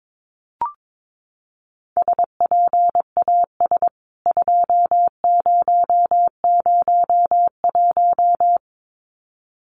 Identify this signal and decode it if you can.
Morse code, heard as a keyed tone, about 22 wpm, 700 Hz. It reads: SPAS 2001
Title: CWT Exchange 22wpm